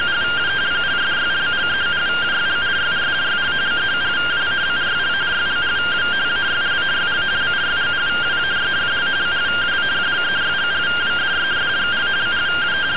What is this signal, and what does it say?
Two-tone teleprinter signal, about 170 Hz shift.
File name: findings / CT AXIAL -142.gen-22 rtty rx